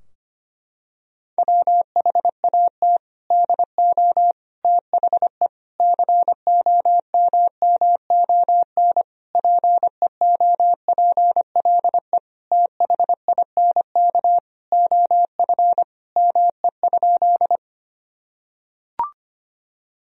Morse code, heard as a keyed tone, about 25 wpm, 700 Hz. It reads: WHAT DO THE COMMON PEOPLE THINK OF ME?
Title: Sentences from Top 1000 Words - Review Part 3 - 25wpm